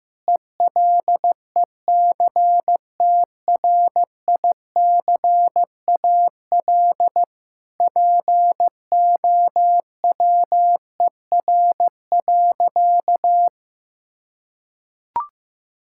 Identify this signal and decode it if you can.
Morse code, heard as a keyed tone, about 15 wpm, 700 Hz. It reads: ELECTRICAL POWER.
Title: Sentences from Top 1000 Words - Review Part 6 - 15wpm